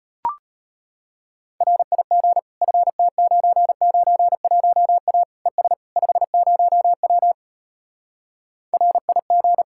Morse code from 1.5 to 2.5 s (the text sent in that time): RIG